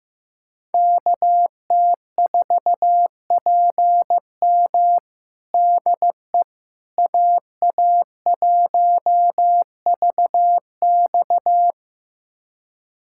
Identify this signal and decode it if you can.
Morse code, heard as a keyed tone, about 15 wpm, 700 Hz. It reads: KT4PM DE AA1VX
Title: USA Callsign DE Callsign - 5 Characters - 15wpm